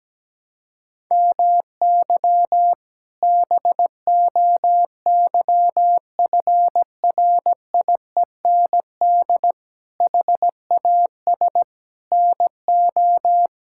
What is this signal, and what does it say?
Morse code, heard as a keyed tone, about 17 wpm, 700 Hz. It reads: MY BOYFRIEND HAS NO
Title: Sentences from Top 1000 Words - Review Part 5 - 17wpm